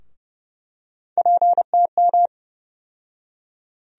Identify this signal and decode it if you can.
Morse code, heard as a keyed tone, about 29 wpm, 700 Hz. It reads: PTM
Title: USA Call Sign Postfixes - 3 Letters 30wpm